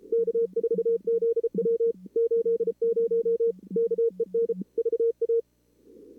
Morse code, heard as a keyed tone, about 34 wpm, 470 Hz. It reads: K4ZW 80 KEN VA
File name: CWSprint / Feb2022 / k5tr-feb-na-sprint-cw-k5ot -2300